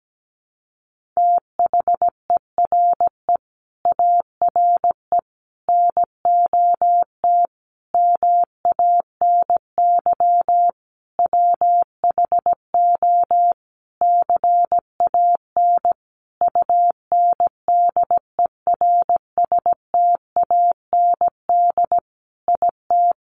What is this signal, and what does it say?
Morse code, heard as a keyed tone, about 17 wpm, 700 Hz. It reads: THERE ARE NOT MANY WHO CAN UNDERSTAND IT
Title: Sentences from Top 1300 Words 17wpm